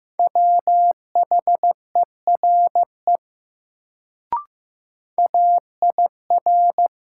Morse code, heard as a keyed tone, about 15 wpm, 700 Hz. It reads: WHERE AIR
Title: Sets of 2 Words - New Word Drill - 15wpm